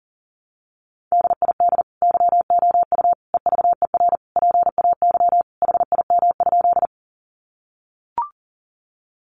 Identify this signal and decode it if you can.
Morse code, heard as a keyed tone, about 40 wpm, 700 Hz. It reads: DID YOU EVER PAY HIM?
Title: Sentences from Top 700 Words 40wpm